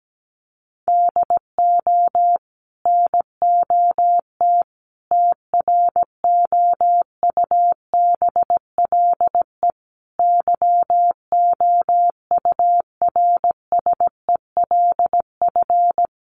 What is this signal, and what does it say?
Morse code, heard as a keyed tone, about 17 wpm, 700 Hz. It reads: DO NOT TROUBLE YOURSELF
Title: Sentences from Top 1200 Words 17wpm